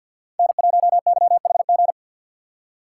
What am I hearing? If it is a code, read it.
Morse code, heard as a keyed tone, about 51 wpm, 700 Hz. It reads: N1YHG
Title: USA Call Signs - 5 Characters - 2x Character Spacing - 50wpm